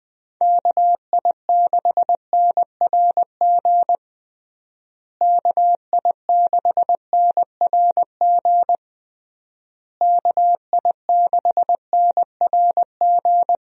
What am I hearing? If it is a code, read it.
Morse code, heard as a keyed tone, about 20 wpm, 700 Hz. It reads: KI6NRG KI6NRG KI6NRG
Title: USA Call Signs - 6 Characters - Repeated 3x - 20wpm